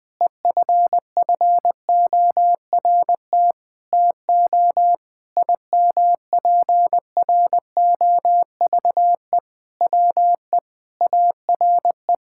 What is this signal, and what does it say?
Morse code, heard as a keyed tone, about 20 wpm, 700 Hz. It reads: EFFORT TO IMPROVE WE ARE
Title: Sentences with 6-Word Phrases - 20wpm